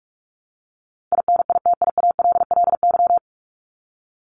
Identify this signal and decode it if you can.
Morse code, heard as a keyed tone, about 44 wpm, 700 Hz. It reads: INITIALLY
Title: Top 1251-1500 Words 45wpm